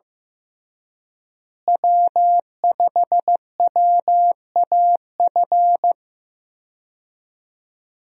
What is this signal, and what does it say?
Morse code, heard as a keyed tone, about 15 wpm, 700 Hz. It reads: W5WAF
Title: USA Call Signs - 5 Characters - Repeated 3x - 15wpm